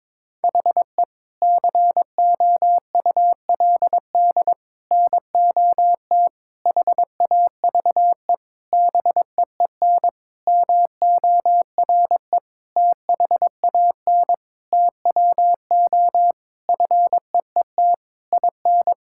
Morse code, heard as a keyed tone, about 22 wpm, 700 Hz. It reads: HE COULD NOT HAVE BEEN MORE THAN TWO FEET IN
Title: Sentences from Top 1600 Words 22wpm